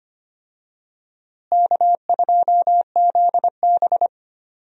Morse code, heard as a keyed tone, about 25 wpm, 700 Hz. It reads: K2ZB
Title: USA Call Signs - 4 Characters - Repeated 3x - 25wpm